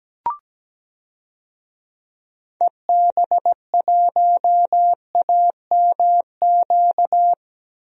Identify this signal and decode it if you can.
Morse code, heard as a keyed tone, about 17 wpm, 700 Hz. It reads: EB1AMQ
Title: DXCC-100 Call Signs - 6 Characters - 17wpm